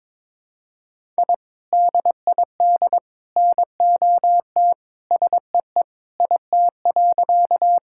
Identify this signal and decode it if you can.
Morse code, heard as a keyed tone, about 22 wpm, 700 Hz. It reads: I DID NOT SEE IT.
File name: practice / 22wpm / Sentences with 3-Word Phrases - 22wpm